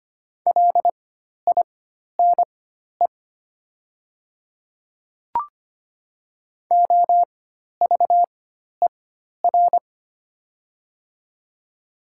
Morse code, heard as a keyed tone, about 25 wpm, spaced as for 12 wpm, 700 Hz. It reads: LINE OVER
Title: 4x Character Spacing - Top 200 Words - 25wpm